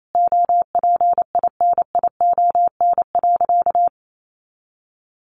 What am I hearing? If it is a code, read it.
Morse code, heard as a keyed tone, about 28 wpm, 700 Hz. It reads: OPINION.
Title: A Sound Of Thunder 28wpm